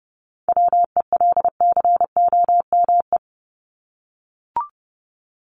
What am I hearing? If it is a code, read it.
Morse code, heard as a keyed tone, about 30 wpm, 700 Hz. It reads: WELCOME
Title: Top 1751-2000 Words 30wpm